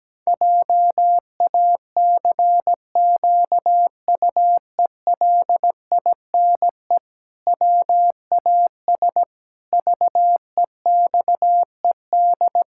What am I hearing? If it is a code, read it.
Morse code, heard as a keyed tone, about 17 wpm, 700 Hz. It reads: JACQUELINE WAS VEXED
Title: Pangram Challenge v2 - 17wpm